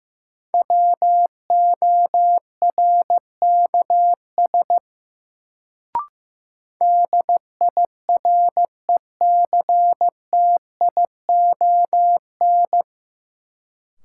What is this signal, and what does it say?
Morse code, heard as a keyed tone, about 15 wpm, 700 Hz. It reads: WORKS DIRECTION
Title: Sentences from Top 1000 Words - Review Part 5 - New Word Drill - 15wpm